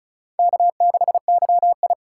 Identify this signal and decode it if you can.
Morse code, heard as a keyed tone, about 36 wpm, 700 Hz. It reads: K6YI
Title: USA Call Signs - 4 Characters - 4x Character Spacing - 35wpm